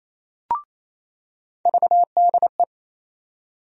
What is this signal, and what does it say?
Morse code, heard as a keyed tone, about 28 wpm, 700 Hz. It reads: VDE
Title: USA Call Sign Postfixes - 3 Letters 28wpm